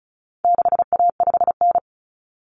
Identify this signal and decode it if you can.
Morse code, heard as a keyed tone, about 34 wpm, 700 Hz. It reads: 6A5N
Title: Common Cut Numbers - 4-Digit 35wpm